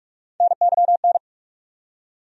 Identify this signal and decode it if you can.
Morse code, heard as a keyed tone, about 47 wpm, 700 Hz. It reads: NYN
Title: USA Call Sign Postfixes - 3 Letters 45wpm